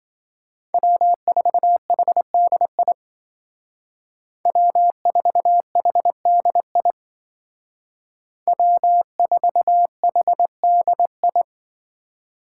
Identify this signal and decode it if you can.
Morse code, heard as a keyed tone, about 26 wpm, 700 Hz. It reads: W4HDI W4HDI W4HDI